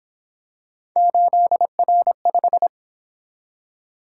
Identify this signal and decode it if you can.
Morse code, heard as a keyed tone, about 26 wpm, 700 Hz. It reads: 8R5
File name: Speed-Racing - Three Letter-Number Combinations - 22wpm